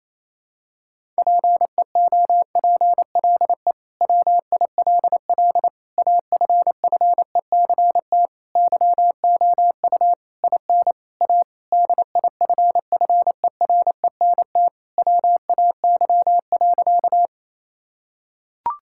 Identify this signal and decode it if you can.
Morse code, heard as a keyed tone, about 28 wpm, 700 Hz. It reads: PEOPLE WILL AFFECT YOU IN A DIFFERENT WAY.
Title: Sentences from Top 1000 Words - Review Part 6 - 28wpm